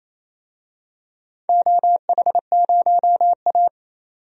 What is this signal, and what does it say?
Morse code, heard as a keyed tone, about 28 wpm, 700 Hz. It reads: OH0A
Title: DXCC-100 Call Signs - 4 Characters - 28wpm